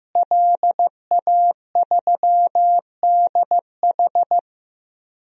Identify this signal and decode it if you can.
Morse code, heard as a keyed tone, about 15 wpm, 700 Hz. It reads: LA3DH
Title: DXCC-100 Call Signs - 5 Characters - Repeated 2x - 15wpm